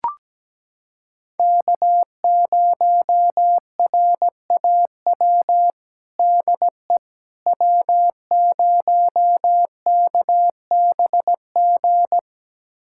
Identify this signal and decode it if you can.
Morse code, heard as a keyed tone, about 17 wpm, 700 Hz. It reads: K0RAW DE W0KBG
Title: USA Callsign DE Callsign - 5 Characters - 17wpm